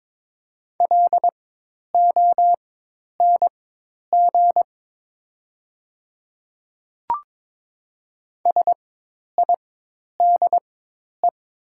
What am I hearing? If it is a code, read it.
Morse code, heard as a keyed tone, about 22 wpm, spaced as for 10 wpm, 700 Hz. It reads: LONG SIDE
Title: 4x Character Spacing - Top 100 Words - 22wpm